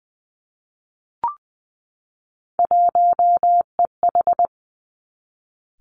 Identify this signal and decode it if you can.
Morse code, heard as a keyed tone, about 20 wpm, 700 Hz. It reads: 1EH
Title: Three Challenging Letter-Number Combinations 20wpm